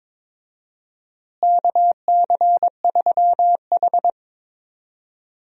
Morse code, heard as a keyed tone, about 21 wpm, 700 Hz. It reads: KC3H
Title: USA Call Signs - 4 Characters - Repeated 2x - 22wpm